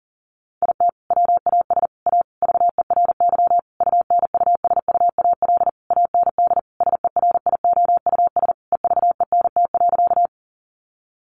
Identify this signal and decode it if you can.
Morse code, heard as a keyed tone, about 40 wpm, 700 Hz. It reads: IT WAS A VERY UNUSUAL AND SERIOUS EVENT.
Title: Sentences from Top 1400 Words 40wpm